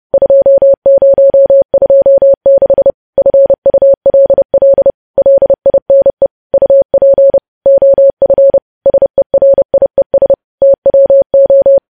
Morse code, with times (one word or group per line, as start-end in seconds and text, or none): 0.1-2.9 2026
3.2-4.9 FULL
5.2-6.3 LINE
6.5-7.4 UP
7.7-8.6 OF
8.9-10.3 SERIES
10.6-11.8 TWO